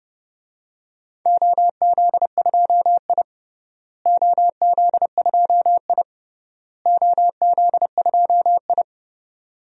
Morse code, heard as a keyed tone, about 30 wpm, 700 Hz. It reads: OZ2I OZ2I OZ2I
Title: DXCC-100 Call Signs - 4 Characters - Repeated 3x - 30wpm